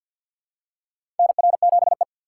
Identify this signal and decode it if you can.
Morse code, heard as a keyed tone, about 47 wpm, 690 Hz. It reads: NR7E
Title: USA Call Signs - 4 Characters - 6x Character Spacing - 50wpm